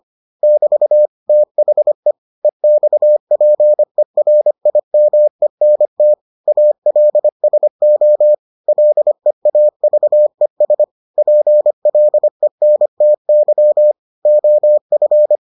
Morse code, heard as a keyed tone, about 25 wpm, 600 Hz. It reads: <BT> THE EXPERIMENT ALSO LEAVES PLENTY OF